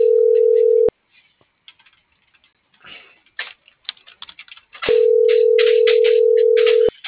Ringback tone (North America), 440 + 480 Hz.
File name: CENTRAL MABE / out xquenda.trinidad